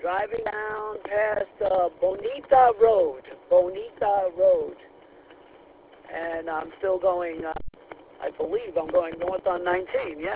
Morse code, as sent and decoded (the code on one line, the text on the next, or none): none